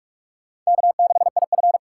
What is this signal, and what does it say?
Morse code, heard as a keyed tone, about 44 wpm, 700 Hz. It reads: K6IF